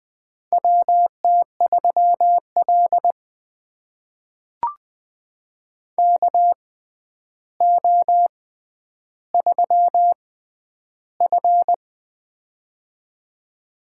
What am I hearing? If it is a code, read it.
Morse code, heard as a keyed tone, about 20 wpm, spaced as for 10 wpm, 700 Hz. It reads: WT3L KO3F
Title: USA Call Signs - 4 Characters - 6x Character Spacing - 20wpm